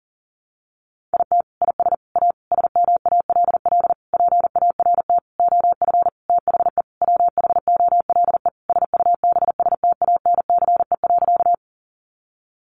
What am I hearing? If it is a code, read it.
Morse code, heard as a keyed tone, about 40 wpm, 700 Hz. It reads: IT IS A SMALL PART OF THE WHOLE SUBSTANCE.
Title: Sentences from Top 1500 Words 40wpm